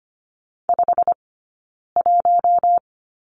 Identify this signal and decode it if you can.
Morse code, heard as a keyed tone, about 25 wpm, 700 Hz. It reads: 5 1
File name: course / 25:10wpm / 092 Lesson 16d - Letter c - Two Characters 25wpm